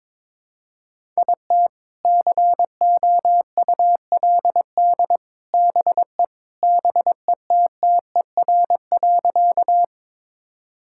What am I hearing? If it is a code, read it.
Morse code, heard as a keyed tone, about 21 wpm, 700 Hz. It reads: IT COULD BE BETTER.
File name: practice / 22wpm / Sentences with 2-Word Phrases - 22wpm